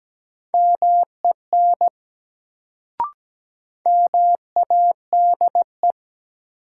Morse code, heard as a keyed tone, about 17 wpm, 700 Hz. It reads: MEN MADE
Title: Sentences from Top 200 Words - New Word Drill - 17wpm